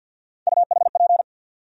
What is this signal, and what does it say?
Morse code, heard as a keyed tone, about 49 wpm, 700 Hz. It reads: UHP
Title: USA Call Sign Postfixes - 3 Letters 50wpm